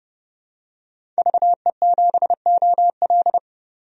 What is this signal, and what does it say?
Morse code, heard as a keyed tone, about 30 wpm, 700 Hz. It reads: VE7OL
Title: DXCC-100 Call Signs - 5 Characters - 30wpm